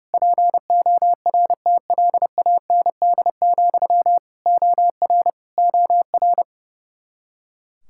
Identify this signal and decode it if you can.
Morse code, heard as a keyed tone, about 30 wpm, 700 Hz. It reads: PORTLAND, OR OR
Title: Practice QSO - Exchange 1 - Beginning - 30wpm